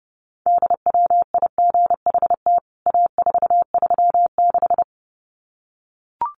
Morse code, heard as a keyed tone, about 30 wpm, 700 Hz. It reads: DWIGHT A436
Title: CWT Exchange 30wpm